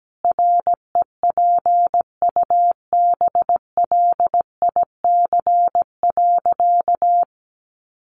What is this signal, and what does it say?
Morse code, heard as a keyed tone, about 17 wpm, 700 Hz. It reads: REPUBLIC.
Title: Sentences from Top 1500 Words 17wpm